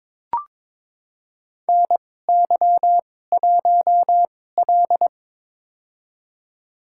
Morse code, heard as a keyed tone, about 22 wpm, spaced as for 16 wpm, 700 Hz. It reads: NY1L
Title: USA Call Signs - 4 Characters - 2x Character Spacing - 22wpm